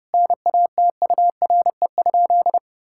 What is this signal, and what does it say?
Morse code, heard as a keyed tone, about 30 wpm, 700 Hz. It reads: NATURE?
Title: Sentences from Top 1000 Words - Review Part 6 - 30wpm